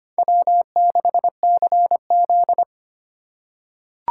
Morse code, heard as a keyed tone, about 25 wpm, 700 Hz. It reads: W6CZ
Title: USA Call Signs - 4 Characters - 8x Character Spacing - 25wpm